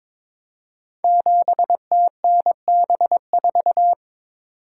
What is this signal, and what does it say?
Morse code, heard as a keyed tone, about 22 wpm, 700 Hz. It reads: 7TNB4
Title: Code Groups - 5 Letter:Numbers - 22wpm